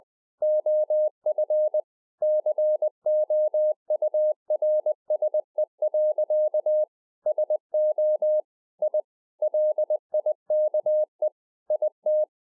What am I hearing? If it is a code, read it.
Morse code, heard as a keyed tone, about 20 wpm, 610 Hz. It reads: OF COURSE. SO I LIKE IT
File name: 2011-04-mp3 / A SMALL TRANCEIVER 3 20WPM